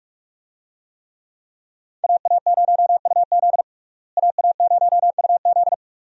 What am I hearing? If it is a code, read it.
Morse code, heard as a keyed tone, about 47 wpm, 700 Hz. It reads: AA0UZ AA0UZ
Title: USA Call Signs - 5 Characters - Repeated 2x - 45wpm